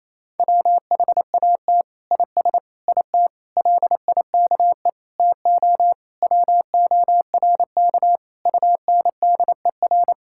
Morse code, heard as a keyed tone, about 28 wpm, 700 Hz. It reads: WHAT IS IT LIKE TO WORK UNDER
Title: Sentences from Top 1000 Words - Review Part 6 - 28wpm